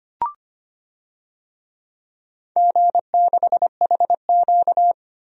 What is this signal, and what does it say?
Morse code, heard as a keyed tone, about 25 wpm, 700 Hz. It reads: G6HQ
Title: DXCC-100 Call Signs - 4 Characters - 25wpm